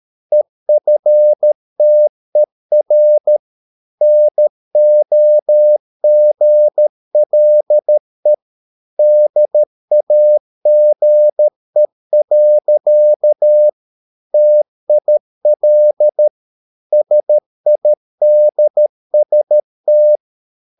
Morse code, heard as a keyed tone, about 13 wpm, 600 Hz. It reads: EFTER NOGLE DAGE. TIL SIDST
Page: Ferie 13 wpm